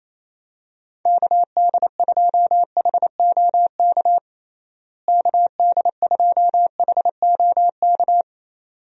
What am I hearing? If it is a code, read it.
Morse code, heard as a keyed tone, about 27 wpm, 700 Hz. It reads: KD2HOK KD2HOK
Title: USA Call Signs - 6 Characters - Repeated 2x - 28wpm